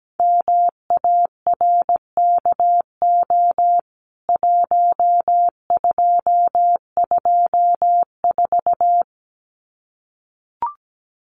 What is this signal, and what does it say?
Morse code, heard as a keyed tone, about 17 wpm, 700 Hz. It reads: MARKO 1224
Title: CWT Exchange 17wpm